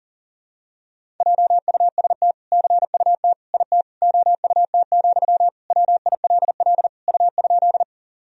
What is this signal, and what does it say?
Morse code, heard as a keyed tone, about 37 wpm, 700 Hz. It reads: JUST CUT IT OUT, WILL U?
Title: Sentences from Top 300 Words and Abbreviations 40wpm